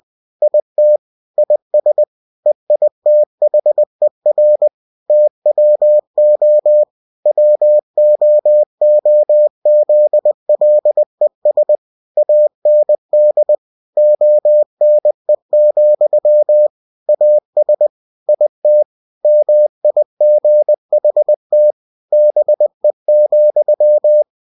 Morse code, heard as a keyed tone, about 20 wpm, 600 Hz. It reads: IT IS EITHER TWO WOOZLES AND ONE, AS IT MIGHT BE,
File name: cw-books / winnie-the-pooh / winnie-20wpm-page-0023